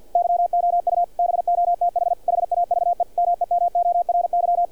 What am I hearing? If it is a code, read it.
Morse code, heard as a keyed tone, about 46 wpm, 700 Hz, with radio noise. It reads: YOU BOTH HAVE MEMORY
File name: archived / CRSnet